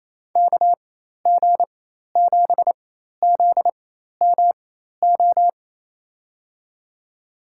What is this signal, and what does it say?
Morse code, heard as a keyed tone, about 28 wpm, spaced as for 13 wpm, 700 Hz. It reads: KG7ZMO
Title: USA Call Signs - 6 Characters - 4x Character Spacing - 28wpm